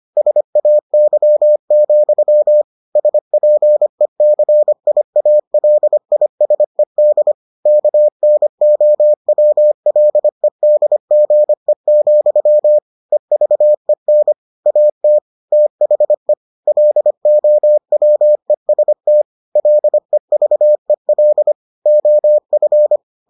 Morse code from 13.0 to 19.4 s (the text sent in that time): EVEN AT THE LOWEST